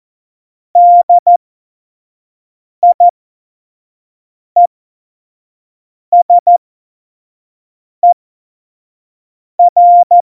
Morse code, heard as a keyed tone, about 14 wpm, spaced as for 5 wpm, 700 Hz. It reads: DIESER
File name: Morsekurs / Witze / Witz 02. Juli 2019, 25 F70